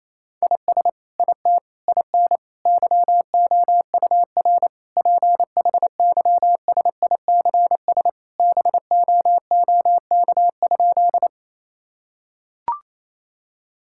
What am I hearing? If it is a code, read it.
Morse code, heard as a keyed tone, about 28 wpm, 700 Hz. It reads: IS IT IN YOUR PHYSICS BOOK?
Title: Sentences from Top 1000 Words - Review Part 5 - 28wpm